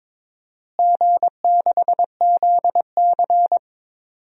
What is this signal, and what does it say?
Morse code, heard as a keyed tone, about 22 wpm, 700 Hz. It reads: G6ZC
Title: DXCC-100 Call Signs - 4 Characters - Repeated 2x - 22wpm